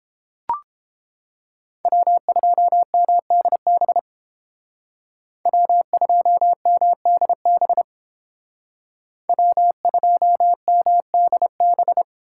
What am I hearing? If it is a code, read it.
Morse code, heard as a keyed tone, about 30 wpm, 700 Hz. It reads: W2MDB W2MDB W2MDB